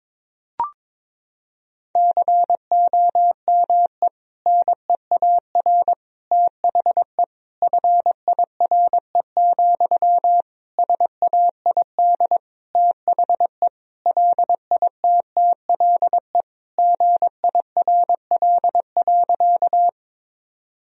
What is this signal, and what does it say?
Morse code, heard as a keyed tone, about 22 wpm, 700 Hz. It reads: COME NEAR THE FIRE, SAID THE LITTLE GIRL.
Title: Sentences from Top 1000 Words - Review Part 1 - 22wpm